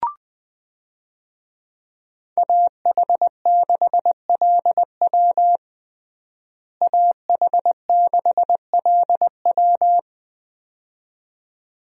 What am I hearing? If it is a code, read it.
Morse code, heard as a keyed tone, about 20 wpm, 700 Hz. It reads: AH6LW AH6LW
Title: USA Call Signs - 5 Characters - Repeated 2x - 20wpm